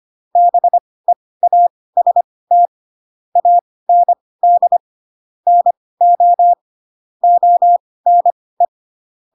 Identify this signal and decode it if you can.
Morse code, heard as a keyed tone, about 25 wpm, spaced as for 18 wpm, 700 Hz. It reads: BEAST AND NO ONE
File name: Pinocchio / Pinocchio ch2 12 25 ik1hge